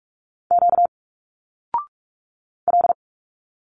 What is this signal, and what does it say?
Morse code, heard as a keyed tone, about 46 wpm, 700 Hz. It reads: Q L
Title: Single Letters 45wpm